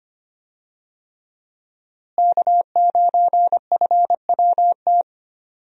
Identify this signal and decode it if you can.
Morse code, heard as a keyed tone, about 25 wpm, 700 Hz. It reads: K9FWT
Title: USA Call Signs - 5 Characters - Repeated 3x - 25wpm